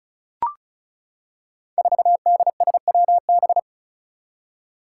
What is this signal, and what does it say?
Morse code, heard as a keyed tone, about 36 wpm, 700 Hz. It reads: 4DSWB
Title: Five Challenging Letter-Number Combinations 35wpm